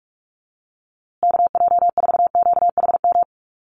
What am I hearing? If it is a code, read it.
Morse code, heard as a keyed tone, about 44 wpm, 700 Hz. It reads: KJ4QHM
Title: USA Call Signs Repeated 2x 45wpm